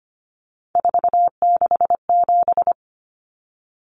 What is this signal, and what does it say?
Morse code, heard as a keyed tone, about 25 wpm, 700 Hz. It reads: 467
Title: Common Cut Numbers - 3-Digit 25wpm